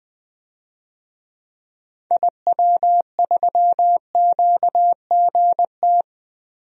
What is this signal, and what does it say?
Morse code, heard as a keyed tone, about 20 wpm, 700 Hz. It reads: IW3QGT
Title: DXCC-100 Call Signs - 6 Characters - 20wpm